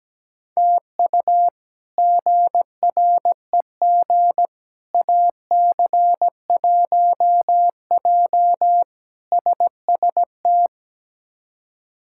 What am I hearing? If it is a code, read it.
Morse code, heard as a keyed tone, about 17 wpm, 700 Hz. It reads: TU GREG AC1J SST
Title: K1USN SST QSO Practice - 17wpm